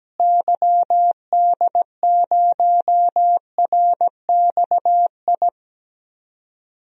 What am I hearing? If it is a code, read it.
Morse code, heard as a keyed tone, about 17 wpm, 700 Hz. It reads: YD0RXI
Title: DXCC-100 Call Signs - 6 Characters - Repeated 2x - 17wpm